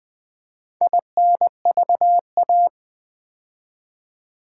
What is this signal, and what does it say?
Morse code, heard as a keyed tone, about 20 wpm, 700 Hz. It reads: INVA
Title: Build A Word - 4 Letters At A Time - 8-Letter Words - 20wpm